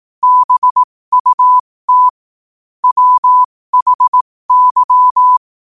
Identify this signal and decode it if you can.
Morse code, heard as a keyed tone, about 18 wpm, spaced as for 15 wpm, 1000 Hz. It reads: BUT WHY
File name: qotdfiles / 15wpm